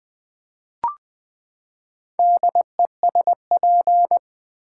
Morse code, heard as a keyed tone, about 20 wpm, 700 Hz. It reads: DESP
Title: Build A Word - 4 Letters At A Time - 11-Letter Words - 20wpm